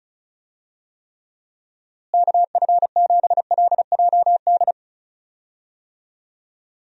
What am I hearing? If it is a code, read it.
Morse code, heard as a keyed tone, about 34 wpm, 700 Hz. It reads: KF7LJD
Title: USA Call Signs - 6 Characters - Repeated 2x - 35wpm